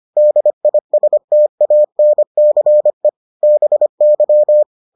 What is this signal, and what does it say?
Morse code, heard as a keyed tone, about 25 wpm, 600 Hz. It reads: DISTANCE BY